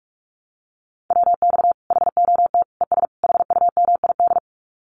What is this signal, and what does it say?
Morse code, heard as a keyed tone, about 44 wpm, 700 Hz. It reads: WX HOT ES HUMID
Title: Common QSO Phrases 45wpm